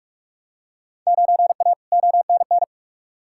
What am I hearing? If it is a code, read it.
Morse code, heard as a keyed tone, about 46 wpm, 700 Hz. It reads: 9A ONN
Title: Field Day Exchanges Encore - 45wpm